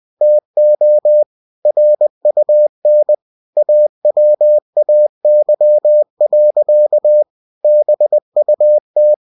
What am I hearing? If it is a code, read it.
Morse code, heard as a keyed tone, about 20 wpm, 600 Hz. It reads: TO RUN AWAY. BUT
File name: cw-books / winnie-the-pooh / winnie-20wpm-page-0042